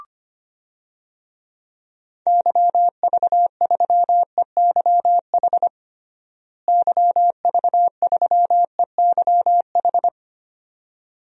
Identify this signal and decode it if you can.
Morse code, heard as a keyed tone, about 25 wpm, 700 Hz. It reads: YV3EYH YV3EYH
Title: DXCC-100 Call Signs - 6 Characters - Repeated 2x - 25wpm